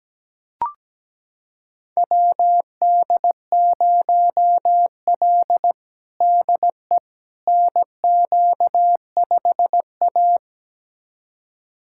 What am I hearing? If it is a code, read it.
Morse code, heard as a keyed tone, about 17 wpm, 700 Hz. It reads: WD0L DE NQ5A
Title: USA Callsign DE Callsign - 4 Characters - 17wpm